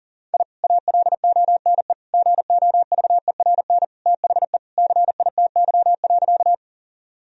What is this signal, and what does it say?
Morse code, heard as a keyed tone, about 40 wpm, 700 Hz. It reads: I ALONE GOVERN THE CITY.
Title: Sentences from Top 1000 Words - Review Part 2 - 40wpm